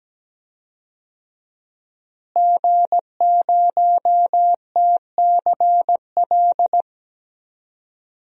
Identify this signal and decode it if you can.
Morse code, heard as a keyed tone, about 17 wpm, 700 Hz. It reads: G0TCL
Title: DXCC-100 Call Signs - 5 Characters - Repeated 2x - 17wpm